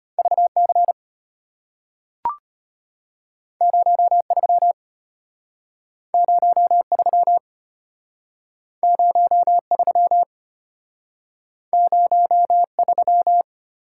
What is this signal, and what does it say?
Morse code, heard as a keyed tone, about 34 wpm, 700 Hz. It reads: VC 03 03 03 03